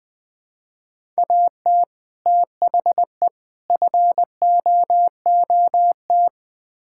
Morse code, heard as a keyed tone, about 20 wpm, 700 Hz. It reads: AT THE FOOT
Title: Build A Phrase - 5-Word Phrases - 20wpm